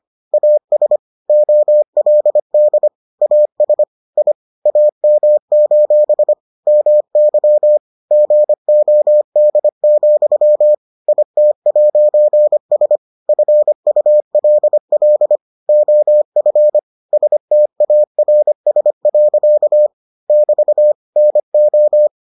Morse code, heard as a keyed tone, about 25 wpm, 600 Hz. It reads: AS OLD AS I AM: MY GOD, IT'S FULL OF STARS. <BT> NO